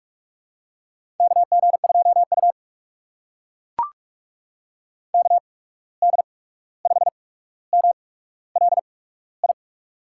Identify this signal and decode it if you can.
Morse code, heard as a keyed tone, about 44 wpm, spaced as for 17 wpm, 700 Hz. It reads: KG2U KD5MLI
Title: USA Call Signs - 8x Character Spacing - 45wpm